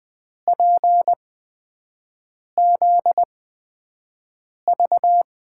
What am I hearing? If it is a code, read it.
Morse code, heard as a keyed tone, about 20 wpm, spaced as for 5 wpm, 700 Hz. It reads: PZV
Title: USA Call Signs - 8x Character Spacing - 20wpm